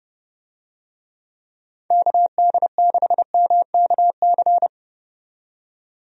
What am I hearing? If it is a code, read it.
Morse code, heard as a keyed tone, about 30 wpm, 700 Hz. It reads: KD6MKC
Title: USA Call Signs - 6 Characters - 30wpm